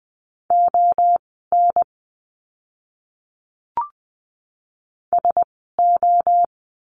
Morse code, heard as a keyed tone, about 20 wpm, spaced as for 14 wpm, 700 Hz. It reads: ON SO
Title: 230 Lesson 37f - Letter x - Words 20wpm